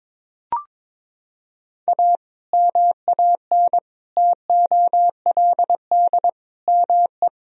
Morse code, heard as a keyed tone, about 22 wpm, 700 Hz. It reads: A MAN TOLD ME
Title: Sentences from Top 1000 Words - Review Part 3 - 22wpm